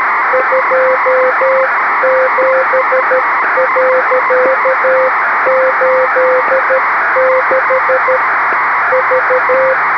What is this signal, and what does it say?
Morse code, heard as a keyed tone, about 13 wpm, 520 Hz, with radio noise. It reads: 27.86 V
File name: arissat-1 OK-ISS 06h54utc by dk3wn